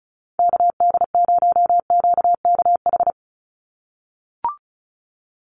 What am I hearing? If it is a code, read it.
Morse code, heard as a keyed tone, about 34 wpm, 700 Hz. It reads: KD0QKH
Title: USA Call Signs 35wpm